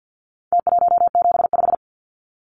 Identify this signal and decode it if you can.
Morse code, heard as a keyed tone, about 49 wpm, 700 Hz. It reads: T175
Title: Numbers - Contest Style - 4-Digit 50wpm